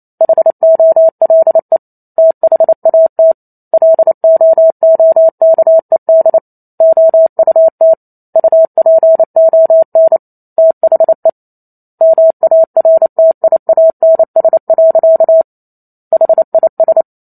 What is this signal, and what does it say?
Morse code, heard as a keyed tone, about 28 wpm, 650 Hz. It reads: HOLE THAT LOOKED OUT UPON THE MARTIANS. HIS